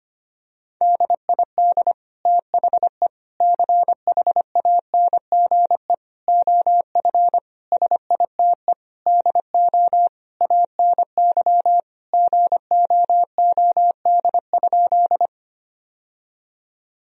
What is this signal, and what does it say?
Morse code, heard as a keyed tone, about 25 wpm, 700 Hz. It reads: DID THE CHANGE OF SITE DO ANY GOOD?
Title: Sentences from Top 1000 Words - Review Part 4 - 25wpm